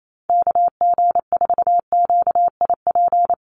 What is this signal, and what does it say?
Morse code, heard as a keyed tone, about 28 wpm, 700 Hz. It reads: KG4QIP
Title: USA Call Signs 28wpm